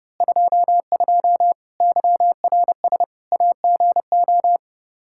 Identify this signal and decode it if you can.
Morse code, heard as a keyed tone, about 30 wpm, 700 Hz. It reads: 22 YRS AGO
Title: Practice QSO - Exchange 3 - More Info - 30wpm